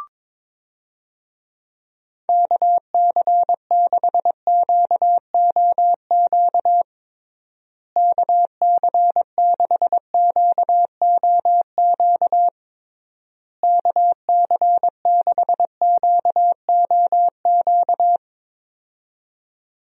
Morse code, heard as a keyed tone, about 22 wpm, 700 Hz. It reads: KC6QOQ KC6QOQ KC6QOQ
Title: USA Call Signs - 6 Characters - Repeated 3x - 22wpm